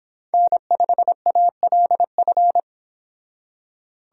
Morse code, heard as a keyed tone, about 28 wpm, 700 Hz. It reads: N5ALF